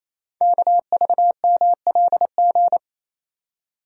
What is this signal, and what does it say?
Morse code, heard as a keyed tone, about 28 wpm, 700 Hz. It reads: KVMLG
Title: Code Groups - 5 Letters - 28wpm